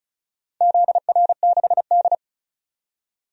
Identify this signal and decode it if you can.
Morse code, heard as a keyed tone, about 36 wpm, 700 Hz. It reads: ZR6D
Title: DXCC-100 Call Signs - 4 Characters - 35wpm